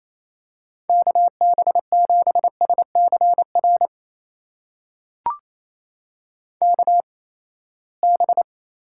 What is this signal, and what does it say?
Morse code, heard as a keyed tone, about 27 wpm, spaced as for 19 wpm, 700 Hz. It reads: KB7SCR KB
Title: USA Call Signs - 6 Characters - 8x Character Spacing - 28wpm